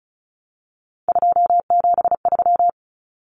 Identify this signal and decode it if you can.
Morse code, heard as a keyed tone, about 34 wpm, 700 Hz. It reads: 273
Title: Common Cut Numbers - 3-Digit 35wpm